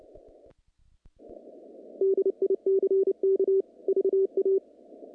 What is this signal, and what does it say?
Morse code, heard as a keyed tone, about 29 wpm, 390 Hz. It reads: DICK VA
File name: contests / naqp-cw-k5tr-op-k5pi- -2200